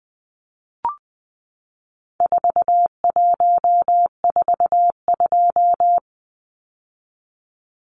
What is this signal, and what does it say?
Morse code, heard as a keyed tone, about 20 wpm, 700 Hz. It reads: 4142
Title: Numbers - 4-Digit 20wpm